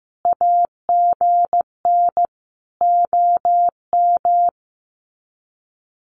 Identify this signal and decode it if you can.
Morse code, heard as a keyed tone, about 15 wpm, 700 Hz. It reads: AGN OM
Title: Common QSO Phrases 15wpm